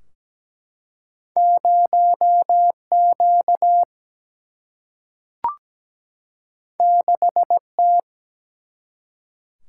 Morse code, heard as a keyed tone, about 17 wpm, 700 Hz. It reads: 0 Q 6 T
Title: Single-Letter Single-Number 17wpm